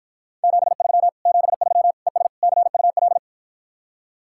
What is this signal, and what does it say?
Morse code, heard as a keyed tone, about 52 wpm, 700 Hz. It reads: 73 73 ES CUL